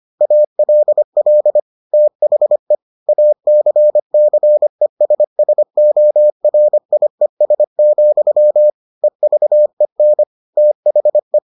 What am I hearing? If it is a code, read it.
Morse code, heard as a keyed tone, about 25 wpm, 600 Hz. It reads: ALL THE ACCESSORIES, EVEN THE